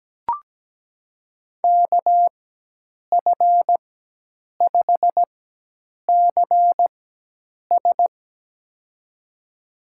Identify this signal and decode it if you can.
Morse code, heard as a keyed tone, about 17 wpm, spaced as for 8 wpm, 700 Hz. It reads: KF5CS
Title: USA Call Signs - 5 Characters - 4x Character Spacing - 17wpm